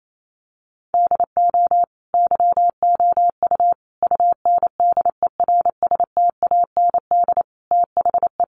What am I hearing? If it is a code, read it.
Morse code, heard as a keyed tone, about 28 wpm, 700 Hz. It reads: DO YOU UNDERSTAND THE
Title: Sentences from Top 1200 Words 28wpm